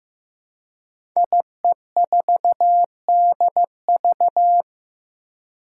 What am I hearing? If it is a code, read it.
Morse code, heard as a keyed tone, about 15 wpm, 700 Hz. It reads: IE4DV
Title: Five Challenging Letter-Number Combinations 15wpm